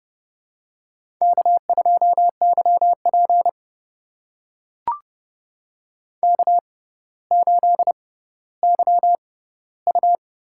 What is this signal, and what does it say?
Morse code, heard as a keyed tone, about 30 wpm, spaced as for 15 wpm, 700 Hz. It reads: K2YP K8YU
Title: USA Call Signs - 4 Characters - 6x Character Spacing - 30wpm